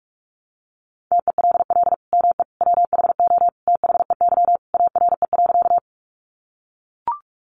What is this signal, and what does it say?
Morse code, heard as a keyed tone, about 44 wpm, 700 Hz. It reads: TELL ME WHO THEY ARE.
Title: Sentences from Top 700 Words 45wpm